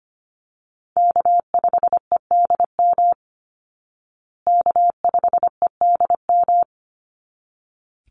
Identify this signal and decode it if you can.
Morse code, heard as a keyed tone, about 25 wpm, 700 Hz. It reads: K5EDM K5EDM
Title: Call Signs Repeated 3x - 25wpm